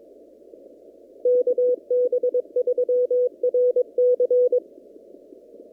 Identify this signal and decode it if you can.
Morse code, heard as a keyed tone, about 22 wpm, 500 Hz. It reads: KB3RC
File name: CW / kt5j wpx cw-op-k5pi